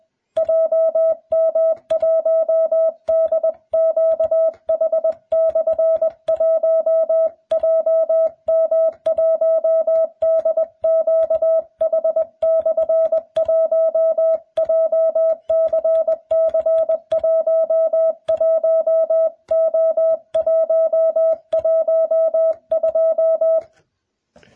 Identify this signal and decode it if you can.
Morse code, heard as a keyed tone, about 20 wpm, 650 Hz. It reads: JM1DQH/1 JM1DQH/1 JCC11O112